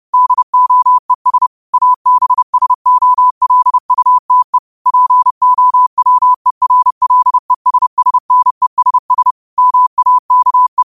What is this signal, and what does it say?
Morse code, heard as a keyed tone, about 30 wpm, 1000 Hz. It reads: DOES ABSOLUTE POWERLESSNESS MAKE
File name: 30wpm